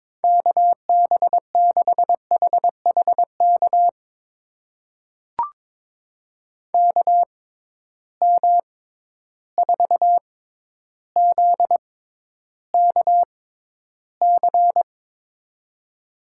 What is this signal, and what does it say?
Morse code, heard as a keyed tone, about 22 wpm, spaced as for 11 wpm, 700 Hz. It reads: KB6HHK KM4ZKC
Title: USA Call Signs - 6 Characters - 6x Character Spacing - 22wpm